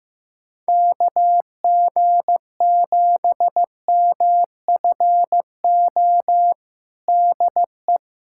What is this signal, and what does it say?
Morse code, heard as a keyed tone, about 16 wpm, 700 Hz. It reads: KG7MFO DE
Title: USA Callsign DE Callsign 15wpm